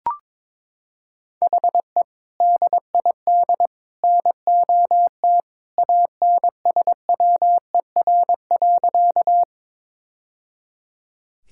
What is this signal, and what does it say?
Morse code, heard as a keyed tone, about 23 wpm, 700 Hz. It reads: HE DID NOT ANSWER.
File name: practice / 22wpm / Sentences with 3-Word Phrases - 22wpm